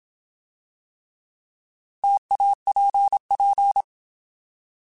Morse code, heard as a keyed tone, about 27 wpm, 800 Hz. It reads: TAPP
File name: Alphabetical / Alphabetical mp3s / Ts @25